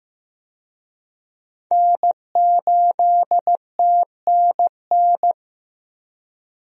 Morse code, heard as a keyed tone, about 15 wpm, 700 Hz. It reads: N8TNN
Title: USA Call Signs - 5 Characters - 15wpm